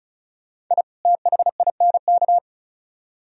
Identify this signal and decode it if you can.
Morse code, heard as a keyed tone, about 44 wpm, 700 Hz. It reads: I THINK